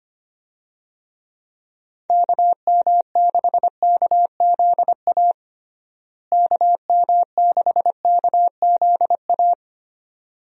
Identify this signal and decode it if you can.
Morse code, heard as a keyed tone, about 25 wpm, 700 Hz. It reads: KM6KZA KM6KZA
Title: USA Call Signs - 6 Characters - Repeated 2x - 25wpm